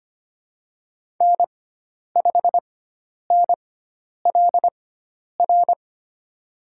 Morse code, heard as a keyed tone, about 25 wpm, spaced as for 10 wpm, 700 Hz. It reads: N5NLR
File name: Lesson 16g - Letter c - Call Signs 25wpm